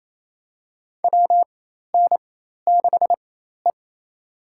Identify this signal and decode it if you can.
Morse code, heard as a keyed tone, about 28 wpm, spaced as for 13 wpm, 700 Hz. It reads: WN6E
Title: USA Call Signs - 4 Characters - 4x Character Spacing - 28wpm